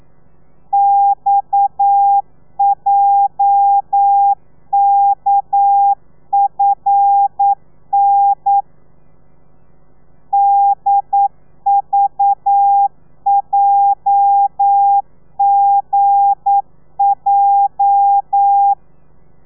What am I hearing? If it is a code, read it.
Morse code, heard as a keyed tone, about 9 wpm, 800 Hz. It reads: XJKFN DVJGJ
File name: telegrafia / i2tfj / Lez8 40